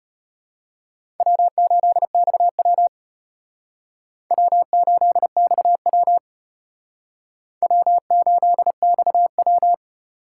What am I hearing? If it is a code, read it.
Morse code, heard as a keyed tone, about 33 wpm, 700 Hz. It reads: W8XW W8XW W8XW